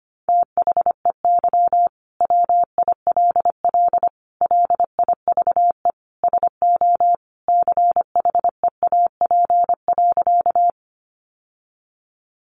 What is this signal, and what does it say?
Morse code, heard as a keyed tone, about 25 wpm, 700 Hz. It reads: THEY WILL LIVE SO CHEAP.
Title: Sentences from Top 1200 Words 25wpm